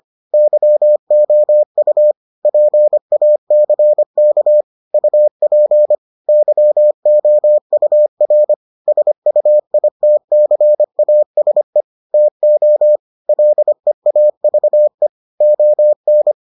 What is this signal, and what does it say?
Morse code, heard as a keyed tone, about 25 wpm, 600 Hz. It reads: YOU PACK UP YOUR SUITCASE TO LEAVE ON